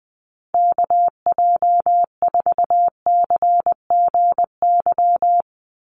Morse code, heard as a keyed tone, about 20 wpm, 700 Hz. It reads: KJ4CGY
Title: USA Call Signs 20wpm